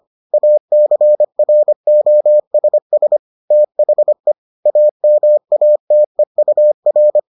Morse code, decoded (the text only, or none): ACROSS THE AMATEUR